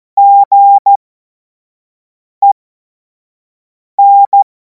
Morse code, heard as a keyed tone, about 14 wpm, spaced as for 5 wpm, 800 Hz. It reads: GEN
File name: Morsekurs / Witze / Witz 24. Februar 2026 25 F70